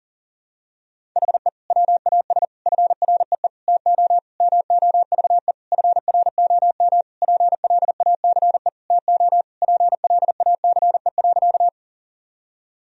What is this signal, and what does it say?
Morse code, heard as a keyed tone, about 40 wpm, 700 Hz. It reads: HE WAS FREE TO MOVE FROM PLACE TO PLACE.
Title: Sentences from Top 1000 Words - Review Part 1 - 40wpm